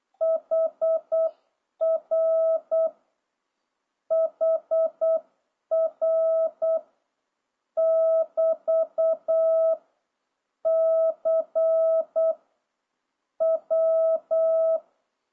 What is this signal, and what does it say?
Morse code, heard as a keyed tone, about 8 wpm, 650 Hz. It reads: HR HR <BT> C W